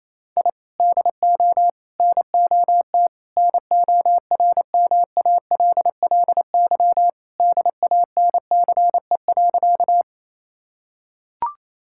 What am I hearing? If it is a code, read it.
Morse code, heard as a keyed tone, about 28 wpm, 700 Hz. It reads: I DO NOT NORMALLY DANCE.
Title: Sentences from Top 1000 Words - Review Part 6 - 28wpm